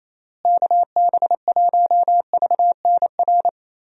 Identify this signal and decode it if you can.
Morse code, heard as a keyed tone, about 28 wpm, 700 Hz. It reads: KB1VNR
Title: USA Call Signs - 6 Characters - Repeated 3x - 28wpm